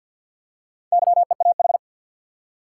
Morse code, heard as a keyed tone, about 51 wpm, 700 Hz. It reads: YEAH
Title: Four Letter Words - 50wpm